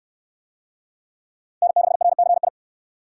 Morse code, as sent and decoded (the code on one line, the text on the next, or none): -. -.... .-. -.-. ..
N6RCI